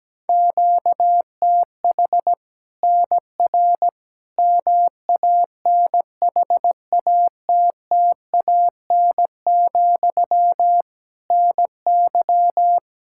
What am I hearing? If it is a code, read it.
Morse code, heard as a keyed tone, about 17 wpm, 700 Hz. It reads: QTH NR MANHATTAN, NY
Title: Practice QSO - Exchange 1 - Beginning - 17wpm